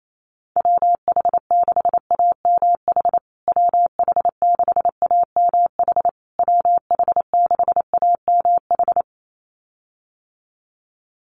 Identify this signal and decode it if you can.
Morse code, heard as a keyed tone, about 28 wpm, 700 Hz. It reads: WH6AMH WH6AMH WH6AMH
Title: USA Call Signs Repeated 3x 28wpm